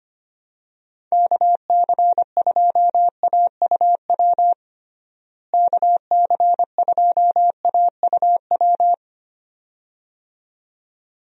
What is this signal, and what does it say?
Morse code, heard as a keyed tone, about 25 wpm, 700 Hz. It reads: KC2AUW KC2AUW
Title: USA Call Signs - 6 Characters - Repeated 2x - 25wpm